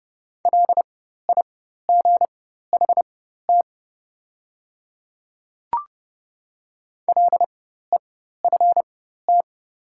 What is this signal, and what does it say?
Morse code, heard as a keyed tone, about 30 wpm, spaced as for 14 wpm, 700 Hz. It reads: LIGHT LEFT
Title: 4x Character Spacing - Top 300 Words - 30wpm